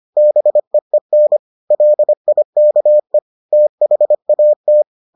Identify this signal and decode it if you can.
Morse code, heard as a keyed tone, about 25 wpm, 600 Hz. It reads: BEEN LIKE THAT